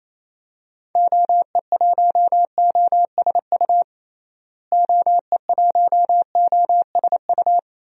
Morse code, heard as a keyed tone, about 27 wpm, 700 Hz. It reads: OE1OSU OE1OSU
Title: DXCC-100 Call Signs - 6 Characters - Repeated 2x - 28wpm